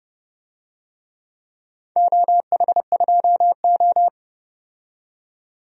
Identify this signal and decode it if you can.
Morse code, heard as a keyed tone, about 30 wpm, 700 Hz. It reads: OH2O
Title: DXCC-100 Call Signs - 4 Characters - 30wpm